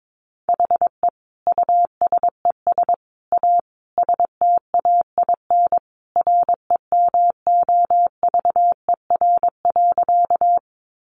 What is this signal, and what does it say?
Morse code, heard as a keyed tone, about 22 wpm, 700 Hz. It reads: HE USES A STAIN REMOVER.
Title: Additional Easy Phrases 22wpm